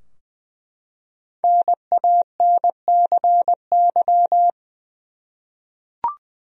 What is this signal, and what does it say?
Morse code, heard as a keyed tone, about 20 wpm, 700 Hz. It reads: NANCY
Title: USA Common Names - Max 5 Letters - 20wpm